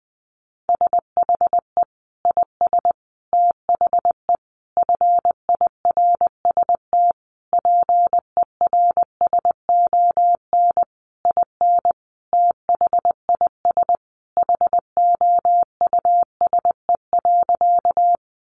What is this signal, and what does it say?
Morse code, heard as a keyed tone, about 20 wpm, 700 Hz. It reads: SHE IS THE FIRST PERSON IN THIS HOUSE.
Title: Sentences from Top 400 Words 20wpm